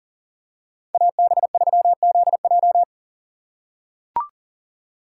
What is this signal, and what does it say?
Morse code, heard as a keyed tone, about 40 wpm, 700 Hz. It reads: AB3ZJ
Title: USA Call Signs - 5 Characters - 6x Character Spacing - 40wpm